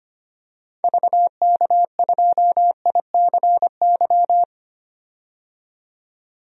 Morse code, heard as a keyed tone, about 25 wpm, 700 Hz. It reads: VK2ICY
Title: DXCC-100 Call Signs - 6 Characters - 25wpm